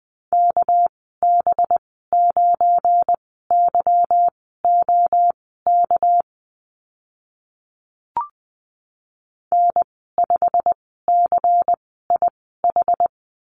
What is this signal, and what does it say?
Morse code, heard as a keyed tone, about 20 wpm, spaced as for 14 wpm, 700 Hz. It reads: KB9YOK N5CIH
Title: 207 Lesson 33g - Letter j - Call Signs 20wpm